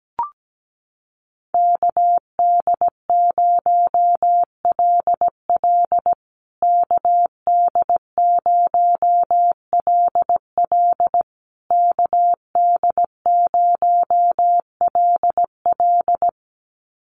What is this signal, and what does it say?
Morse code, heard as a keyed tone, about 17 wpm, 700 Hz. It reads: KD0LL KD0LL KD0LL
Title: USA Call Signs Repeated 3x 17wpm